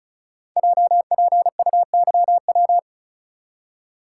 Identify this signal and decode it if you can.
Morse code, heard as a keyed tone, about 36 wpm, 700 Hz. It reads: JPUYW
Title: Code Groups - 5 Letters - 35wpm